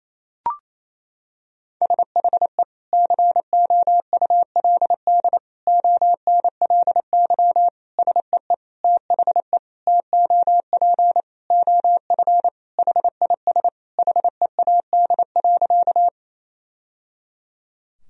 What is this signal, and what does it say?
Morse code, heard as a keyed tone, about 28 wpm, 700 Hz. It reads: SHE COULD ONLY SEE THE TOP OF HIS HEAD.
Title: Sentences from Top 1000 Words - Review Part 2 - 28wpm